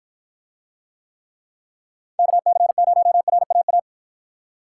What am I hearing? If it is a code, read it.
Morse code, heard as a keyed tone, about 53 wpm, 700 Hz. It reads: KC0RAA